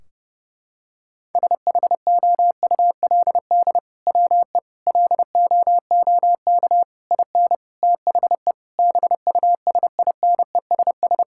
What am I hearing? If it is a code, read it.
Morse code, heard as a keyed tone, about 30 wpm, 700 Hz. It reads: SHOULD WE LOOK IN THE BUSINESS
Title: Sentences from Top 1600 Words 30wpm